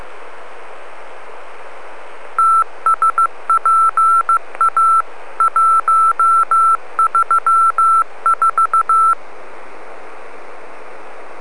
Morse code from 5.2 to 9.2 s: 134